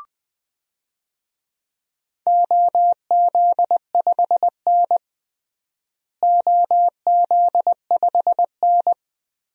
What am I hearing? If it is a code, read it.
Morse code, heard as a keyed tone, about 21 wpm, 700 Hz. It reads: OZ5N OZ5N
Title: DXCC-100 Call Signs - 4 Characters - Repeated 3x - 20wpm